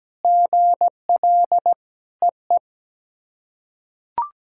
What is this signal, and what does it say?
Morse code, heard as a keyed tone, about 17 wpm, 700 Hz. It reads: GL EE
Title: SOTA QSO Practice 17wpm